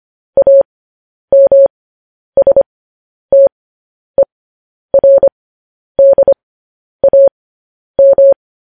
Morse code, heard as a keyed tone, about 25 wpm, spaced as for 10 wpm, 550 Hz. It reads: AMSTERDAM
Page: news-headlines-in-morse-code-at-10-wpm